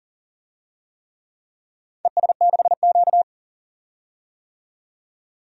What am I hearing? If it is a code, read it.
Morse code, heard as a keyed tone, about 42 wpm, 700 Hz. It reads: ES6Q
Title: DXCC-100 Call Signs - 4 Characters - 40wpm